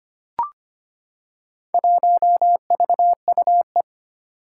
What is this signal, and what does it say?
Morse code, heard as a keyed tone, about 25 wpm, 700 Hz. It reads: 1VUE
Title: Four Challenging Letter-Number Combinations 25wpm